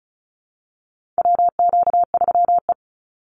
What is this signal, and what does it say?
Morse code, heard as a keyed tone, about 34 wpm, 700 Hz. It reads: WQ3E
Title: Call Signs Repeated 3x - 35wpm